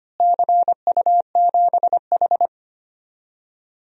Morse code, heard as a keyed tone, about 25 wpm, 700 Hz. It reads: CU7H
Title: DXCC-100 Call Signs - 4 Characters - Repeated 3x - 25wpm